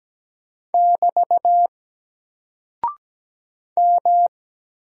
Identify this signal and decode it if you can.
Morse code, heard as a keyed tone, about 17 wpm, 700 Hz. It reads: <BT> M
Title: Single Character (Letter:Number:Punctuation:Prosign) Alt 17wpm